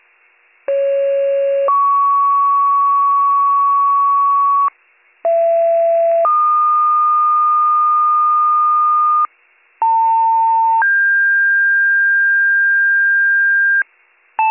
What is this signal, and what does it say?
Two-tone page, heard as a one-second tone, then a three-second tone, about 550 then 1120 Hz, about 670 then 1220 Hz, about 900 then 1600 Hz. It is heard from 0.7 to 13.8 s.